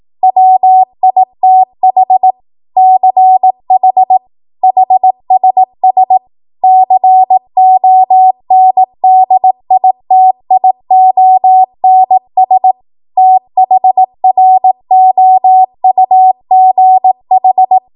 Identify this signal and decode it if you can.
Morse code, heard as a keyed tone, about 18 wpm, 750 Hz. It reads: WITH CH HSS CONDITIONS THROUGH
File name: Morse / 18WPM